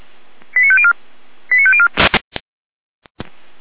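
Five-tone selective call (ZVEI), heard as quick runs of five tones, tones of about 70 ms. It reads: 89504 89504
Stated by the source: ZVEI five-tone selective call received by SDR, recorded by Feuerwehr Orlamünde (fire brigade).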